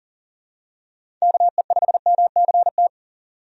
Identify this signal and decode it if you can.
Morse code, heard as a keyed tone, about 40 wpm, 700 Hz. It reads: KE5MCT
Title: USA Call Signs - 6 Characters - 40wpm